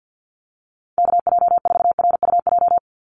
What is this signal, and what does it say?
Morse code, heard as a keyed tone, about 50 wpm, 700 Hz. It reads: KJ4RUJ
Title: USA Call Signs 50wpm